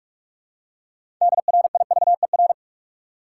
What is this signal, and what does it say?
Morse code, heard as a keyed tone, about 43 wpm, 700 Hz. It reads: DRIVER
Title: Six Letter Words - 45wpm